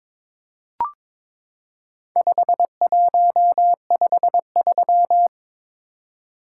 Morse code, heard as a keyed tone, about 22 wpm, 700 Hz. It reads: 5153
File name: practice / 22wpm / Four Letter-Number Combinations - 22wpm